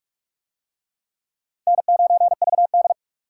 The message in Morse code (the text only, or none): N9VD